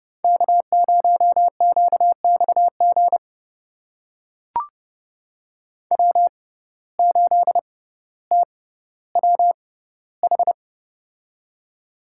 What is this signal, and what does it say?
Morse code, heard as a keyed tone, about 30 wpm, spaced as for 15 wpm, 700 Hz. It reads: K0QXG W8TWH
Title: USA Call Signs - 5 Characters - 6x Character Spacing - 30wpm